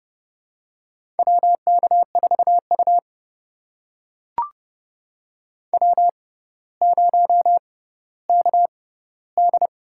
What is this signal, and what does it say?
Morse code, heard as a keyed tone, about 31 wpm, spaced as for 15 wpm, 700 Hz. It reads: WK4U W0KD
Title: USA Call Signs - 4 Characters - 6x Character Spacing - 30wpm